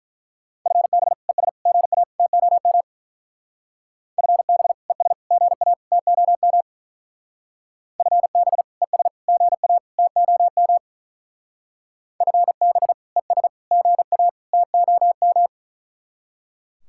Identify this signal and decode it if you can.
Morse code, heard as a keyed tone, about 42 wpm, 700 Hz. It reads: FB ES GA TOM FB ES GA TOM FB ES GA TOM FB ES GA TOM
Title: Speed-Racing Common QSO Phrases Encore - 35wpm